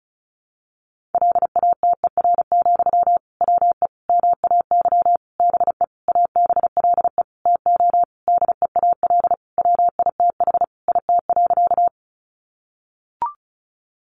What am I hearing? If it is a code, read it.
Morse code, heard as a keyed tone, about 34 wpm, 700 Hz. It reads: LATER, WE MAY BE ABLE TO DEAL WITH IT.
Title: Sentences from Top 800 Words 35wpm